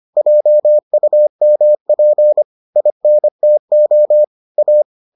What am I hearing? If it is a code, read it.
Morse code, heard as a keyed tone, about 25 wpm, 600 Hz. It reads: JUMP INTO A